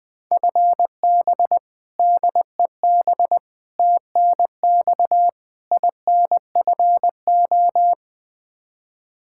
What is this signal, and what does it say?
Morse code, heard as a keyed tone, about 21 wpm, 700 Hz. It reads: FB DEB TNX INFO
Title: Practice QSO - Exchange 3 - More Info - 20wpm